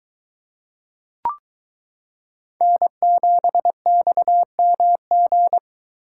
Morse code, heard as a keyed tone, about 22 wpm, 700 Hz. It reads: N7XMG